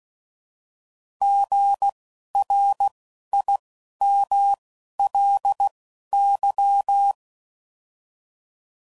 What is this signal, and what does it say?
Morse code, heard as a keyed tone, about 16 wpm, spaced as for 11 wpm, 800 Hz. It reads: GRIMLY
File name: Prefixes / By First Letter